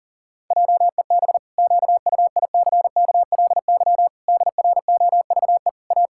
Morse code, heard as a keyed tone, about 40 wpm, 700 Hz. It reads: JEB QUICKLY DROVE A
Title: Pangram Challenge v2 - 40wpm